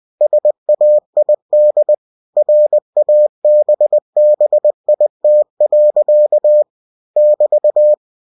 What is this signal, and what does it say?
Morse code, heard as a keyed tone, about 20 wpm, 600 Hz. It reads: SAID RABBIT. <BT>
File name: cw-books / winnie-the-pooh / winnie-20wpm-page-0017